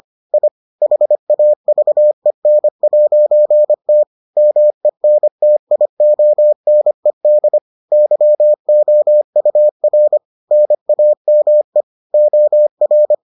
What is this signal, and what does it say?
Morse code, heard as a keyed tone, about 25 wpm, 600 Hz. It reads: I HAVEN'T MENTIONED YOUR NAME OR